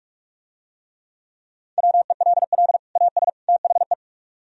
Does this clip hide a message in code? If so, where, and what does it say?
Yes: Morse code from 1.8 to 3.9 s, heard as a keyed tone, about 46 wpm, 700 Hz. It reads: WELL AS THE